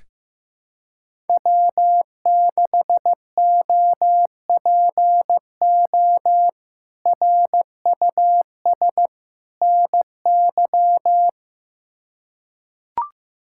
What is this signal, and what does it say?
Morse code, heard as a keyed tone, about 16 wpm, 700 Hz. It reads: W6OPO RUS NY
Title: K1USN SST QSO Practice - 15wpm